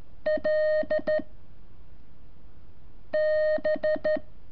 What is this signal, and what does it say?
Morse code, heard as a keyed tone, about 13 wpm, 620 Hz. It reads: L B
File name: curso / ejercicio 26